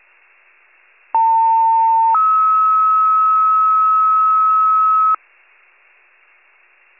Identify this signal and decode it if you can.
Two-tone page, heard as a one-second tone, then a three-second tone, about 900 then 1290 Hz. It is heard from 1.1 to 5.2 s.
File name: pager / 78 RESCUE DUTY / wcalert